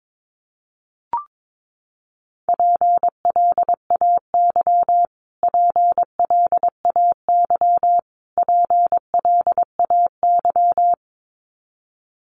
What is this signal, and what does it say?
Morse code, heard as a keyed tone, about 22 wpm, 700 Hz. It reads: PLAY PLAY PLAY
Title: Top 300 Words Repeated 22wpm